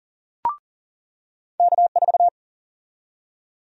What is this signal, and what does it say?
Morse code, heard as a keyed tone, about 40 wpm, 700 Hz. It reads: K4
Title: USA Call Sign Prefixes - 40wpm